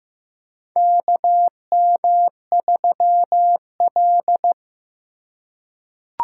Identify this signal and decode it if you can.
Morse code, heard as a keyed tone, about 15 wpm, 700 Hz. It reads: KM3L
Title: USA Call Signs - 4 Characters - 3x Character Spacing - 15wpm